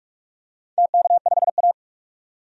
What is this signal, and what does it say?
Morse code, heard as a keyed tone, about 46 wpm, 700 Hz. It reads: TK5A
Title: DXCC-100 Call Signs - 4 Characters - 45wpm